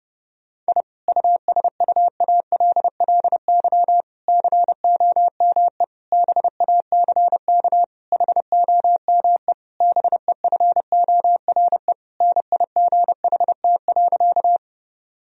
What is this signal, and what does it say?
Morse code, heard as a keyed tone, about 32 wpm, 700 Hz. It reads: I USUALLY COME BACK HOME BEFORE NIGHT.
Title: Sentences from Top 1000 Words - Review Part 3 - 30wpm